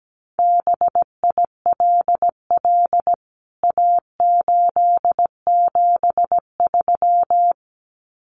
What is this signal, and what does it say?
Morse code, heard as a keyed tone, about 17 wpm, 700 Hz. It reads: BILL A873
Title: CWT Exchange 17wpm